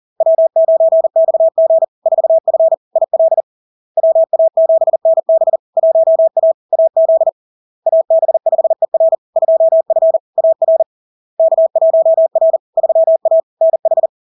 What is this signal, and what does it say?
Morse code, heard as a keyed tone, about 40 wpm, 650 Hz. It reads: W9XG 4F IL WA7NB 1A AZ AB5ER 2F AR K1R 3A NH
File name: LARK-FD-2022-Log-in-CW-40wpm-650-Hz